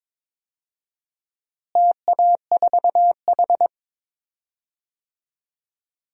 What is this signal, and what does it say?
Morse code, heard as a keyed tone, about 22 wpm, 700 Hz. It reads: TA4H
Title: DXCC-100 Call Signs - 4 Characters - Repeated 3x - 22wpm